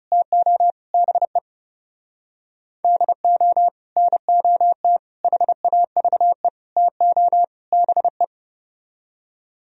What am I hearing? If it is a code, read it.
Morse code, heard as a keyed tone, about 38 wpm, 700 Hz. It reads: TO BE DO NOT HAVE TO BE